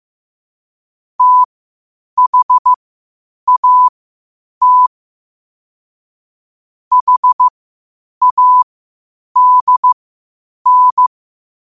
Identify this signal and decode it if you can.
Morse code, heard as a keyed tone, about 15 wpm, spaced as for 8 wpm, 1000 Hz. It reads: THAT HADN
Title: 8 WPM morse code quotes for Thu, 29 Feb 2024